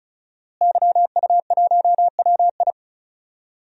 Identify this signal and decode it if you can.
Morse code, heard as a keyed tone, about 36 wpm, 700 Hz. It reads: YU1WI
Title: DXCC-100 Call Signs - 5 Characters - Repeated 2x - 35wpm